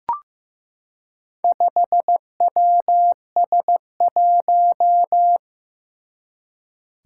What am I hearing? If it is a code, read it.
Morse code, heard as a keyed tone, about 15 wpm, 700 Hz. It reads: 5WS1
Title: Four Challenging Letter-Number Combinations 15wpm